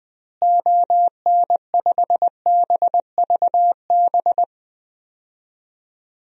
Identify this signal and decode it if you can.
Morse code, heard as a keyed tone, about 20 wpm, 700 Hz. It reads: ON5BVB
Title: DXCC-100 Call Signs - 6 Characters - Repeated 3x - 20wpm